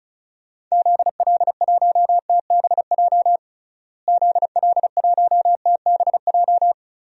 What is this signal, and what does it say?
Morse code, heard as a keyed tone, about 34 wpm, 700 Hz. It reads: ZL1TBJ ZL1TBJ
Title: DXCC-100 Call Signs - 6 Characters - Repeated 2x - 35wpm